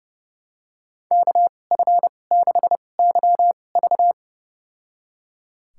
Morse code, heard as a keyed tone, about 30 wpm, spaced as for 22 wpm, 700 Hz. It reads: KF6YV
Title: USA Call Signs - 2x Character Spacing - 30wpm